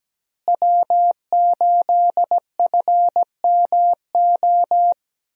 Morse code, heard as a keyed tone, about 17 wpm, 700 Hz. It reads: W8FMO
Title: USA Call Signs - 5 Characters - Repeated 2x - 17wpm